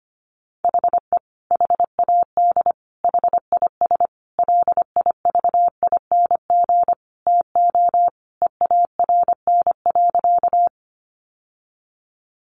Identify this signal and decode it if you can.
Morse code, heard as a keyed tone, about 25 wpm, 700 Hz. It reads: HE HAD HIS LIVING TO EARN.
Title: Sentences from Top 1500 Words 25wpm